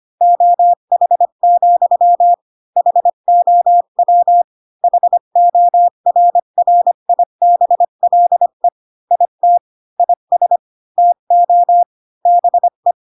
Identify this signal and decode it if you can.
Morse code, heard as a keyed tone, about 25 wpm, 700 Hz. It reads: OH, HOW HORRIBLE IT IS TO BE
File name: Pinocchio / Pinocchio ch5 20 25 ik1hge